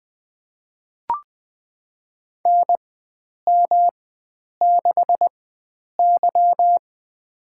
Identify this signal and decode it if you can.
Morse code, heard as a keyed tone, about 20 wpm, spaced as for 9 wpm, 700 Hz. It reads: NM6Y
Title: USA Call Signs - 4 Characters - 4x Character Spacing - 20wpm